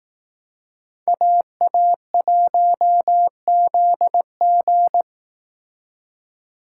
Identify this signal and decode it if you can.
Morse code, heard as a keyed tone, about 18 wpm, 700 Hz. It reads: AA1ZG
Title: Speed Racing - USA 2x2 Call Signs 15wpm